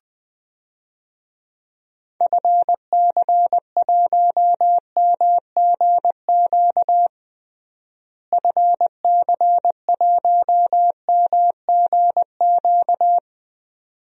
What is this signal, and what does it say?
Morse code, heard as a keyed tone, about 20 wpm, 700 Hz. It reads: FC1MGQ FC1MGQ
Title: DXCC-100 Call Signs - 6 Characters - Repeated 2x - 20wpm